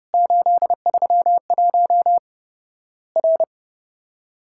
Morse code, heard as a keyed tone, about 28 wpm, 700 Hz. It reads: 831 R
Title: Worked All Europe DX Contest - QTC Exchange - 30wpm